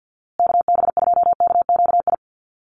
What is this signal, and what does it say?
Morse code, heard as a keyed tone, about 50 wpm, 700 Hz. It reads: KB2KQI
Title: USA Call Signs Repeated 3x 50wpm